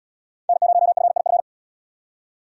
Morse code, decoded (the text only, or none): N0FIF